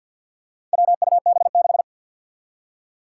Morse code, heard as a keyed tone, about 47 wpm, 700 Hz. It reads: WUB6
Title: Four Challenging Letter-Number Combinations 50wpm